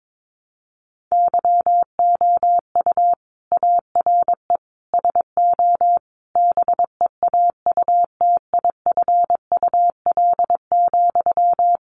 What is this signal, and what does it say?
Morse code, heard as a keyed tone, about 22 wpm, 700 Hz. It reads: YOU ARE SO BEAUTIFUL,
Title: Sentences from Top 1200 Words 22wpm